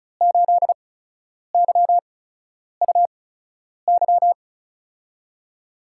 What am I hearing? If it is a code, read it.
Morse code, heard as a keyed tone, about 36 wpm, spaced as for 10 wpm, 700 Hz. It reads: 8YUY
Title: USA Call Signs - 8x Character Spacing - 35wpm